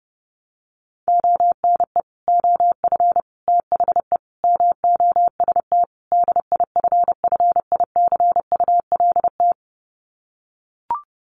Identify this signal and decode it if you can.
Morse code, heard as a keyed tone, about 30 wpm, 700 Hz. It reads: ONE OF THE MOST DIFFICULT
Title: Sets of 5 Words Encore2 30wpm